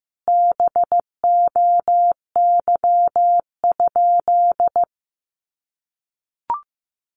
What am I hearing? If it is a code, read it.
Morse code, heard as a keyed tone, about 14 wpm, 700 Hz. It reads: BOY?
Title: Sentences from Top 1000 Words 15wpm